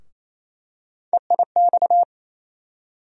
Morse code, heard as a keyed tone, about 28 wpm, 700 Hz. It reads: EIX
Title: USA Call Sign Postfixes - 3 Letters 28wpm